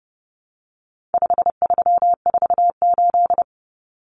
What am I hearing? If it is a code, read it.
Morse code, heard as a keyed tone, about 28 wpm, 700 Hz. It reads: 5348
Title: Common Cut Numbers - 4-Digit 30wpm